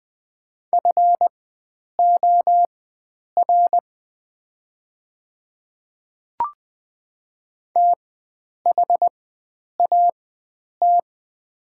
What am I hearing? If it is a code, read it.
Morse code, heard as a keyed tone, about 20 wpm, spaced as for 9 wpm, 700 Hz. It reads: FOR THAT
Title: 4x Character Spacing - Top 100 Words - 20wpm